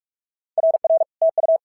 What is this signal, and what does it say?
Morse code, heard as a keyed tone, about 46 wpm, 650 Hz. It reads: RR TU
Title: SOTA QSO Practice 45wpm